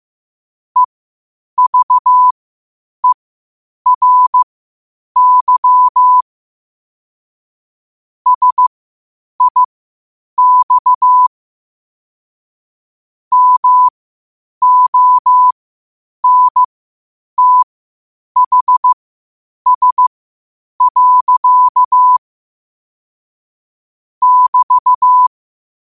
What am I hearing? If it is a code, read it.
Morse code, heard as a keyed tone, about 15 wpm, spaced as for 8 wpm, 1000 Hz. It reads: EVERY SIX MONTHS. <BT>
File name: qotdfiles / 8wpm